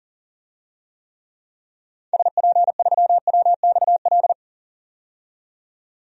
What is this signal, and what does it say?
Morse code, heard as a keyed tone, about 40 wpm, 700 Hz. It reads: SP3WXL
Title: DXCC-100 Call Signs - 6 Characters - 40wpm